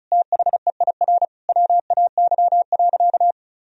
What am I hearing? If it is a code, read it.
Morse code, heard as a keyed tone, about 34 wpm, 700 Hz. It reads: THEIR WAY.
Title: Sentences from Top 1000 Words - Review Part 4 - 35wpm